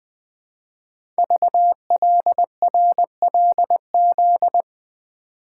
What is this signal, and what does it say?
Morse code, heard as a keyed tone, about 20 wpm, 700 Hz. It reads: VLRLZ
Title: Code Groups - 5 Letters - 20wpm